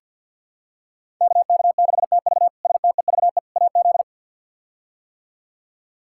Morse code, heard as a keyed tone, about 46 wpm, 700 Hz. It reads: KK6TV STEVE AZ
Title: K1USN SST QSO Practice - 50wpm